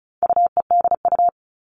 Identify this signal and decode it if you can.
Morse code, heard as a keyed tone, about 34 wpm, 700 Hz. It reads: UEDU
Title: All Cut Numbers - 4-Digit 35wpm